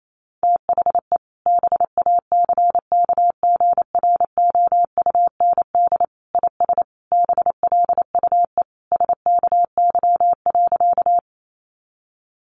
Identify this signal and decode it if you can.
Morse code, heard as a keyed tone, about 28 wpm, 700 Hz. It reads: THE BACKGROUND IS BLUE SKY.
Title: Sentences from Top 1300 Words 28wpm